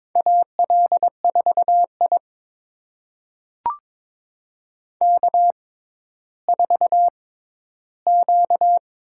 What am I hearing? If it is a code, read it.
Morse code, heard as a keyed tone, about 22 wpm, spaced as for 12 wpm, 700 Hz. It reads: AL4I K4Q
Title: USA Call Signs - 4 Characters - 6x Character Spacing - 22wpm